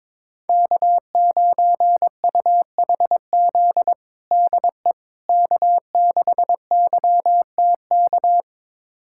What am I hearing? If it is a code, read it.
Morse code, heard as a keyed tone, about 22 wpm, 700 Hz. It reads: K9UHZ DE K6YTK
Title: USA Callsign DE Callsign - 5 Characters - 22wpm